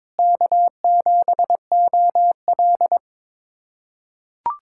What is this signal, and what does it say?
Morse code, heard as a keyed tone, about 22 wpm, 700 Hz. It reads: K7OL
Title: USA Callsign DE Callsign - 4 Characters - 22wpm